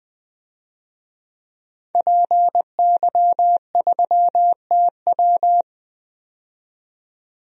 Morse code, heard as a keyed tone, about 20 wpm, 700 Hz. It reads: PY3TW
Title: DXCC-100 Call Signs - 5 Characters - 20wpm